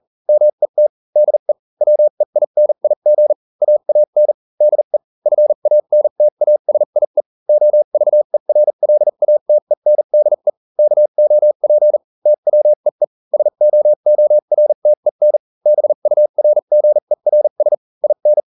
Morse code, heard as a keyed tone, about 40 wpm, spaced as for 35 wpm, 600 Hz. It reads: MET DE WEINIG AAN DE FANTASIE OVERLATENDE KOP TWEE SOORTEN BURGERS IN